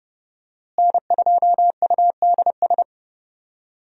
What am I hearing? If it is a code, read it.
Morse code, heard as a keyed tone, about 32 wpm, 700 Hz. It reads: N2UDS
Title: USA Call Signs - 5 Characters - Repeated 2x - 30wpm